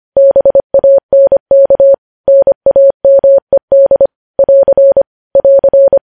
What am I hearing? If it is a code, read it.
Morse code, heard as a keyed tone, about 25 wpm, 550 Hz. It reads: BANK NAMED <AR> <AR>
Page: news headlines in morse code at 25 wpm